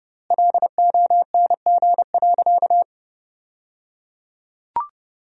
Morse code, heard as a keyed tone, about 30 wpm, 700 Hz. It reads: LONG.
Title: 3x Word-Spacing - Sentences from Top 100 Words - 30wpm